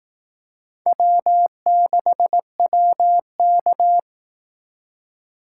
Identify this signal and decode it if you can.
Morse code, heard as a keyed tone, about 18 wpm, 700 Hz. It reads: W6WK
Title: Speed Racing - USA 1x2 Call Signs 15wpm